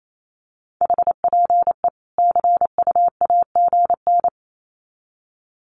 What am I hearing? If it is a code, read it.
Morse code, heard as a keyed tone, about 28 wpm, 700 Hz. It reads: HPE CUAGN
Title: Common QSO Phrases 28wpm